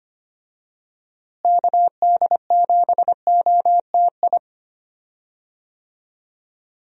Morse code, heard as a keyed tone, about 25 wpm, 700 Hz. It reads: KD7OTI
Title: USA Call Signs - 6 Characters - Repeated 2x - 25wpm